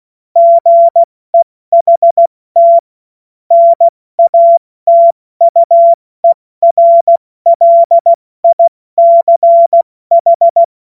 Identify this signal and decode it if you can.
Morse code, heard as a keyed tone, about 16 wpm, spaced as for 14 wpm, 680 Hz. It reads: GEHT NATUERLICH
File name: Morsekurs / Witze / Witz 21. Juni 2022 70 F80